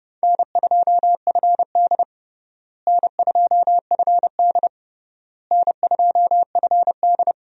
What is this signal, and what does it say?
Morse code, heard as a keyed tone, about 30 wpm, 700 Hz. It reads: N2FD N2FD N2FD
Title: USA Call Signs - 4 Characters - Repeated 3x - 30wpm